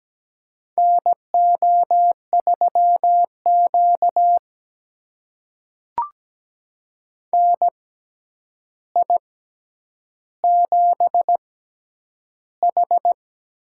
Morse code, heard as a keyed tone, about 17 wpm, spaced as for 9 wpm, 700 Hz. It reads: NO3Q NI7H
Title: USA Call Signs - 4 Characters - 6x Character Spacing - 17wpm